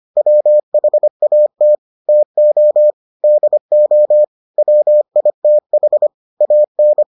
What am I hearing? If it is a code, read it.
Morse code, heard as a keyed tone, about 25 wpm, 600 Hz. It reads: WHAT TO DO WITH AN